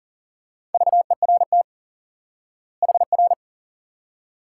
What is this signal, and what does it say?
Morse code, heard as a keyed tone, about 40 wpm, 700 Hz. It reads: VERT HR
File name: IQE Course 3 - Lesson 89 - YAGI - Two Elements - 40wpm